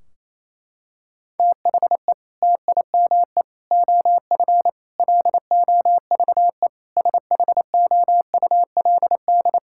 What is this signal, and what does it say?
Morse code, heard as a keyed tone, about 28 wpm, 700 Hz. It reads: THE TIME OF LOVE SHOULD
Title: Sentences from Top 1600 Words 28wpm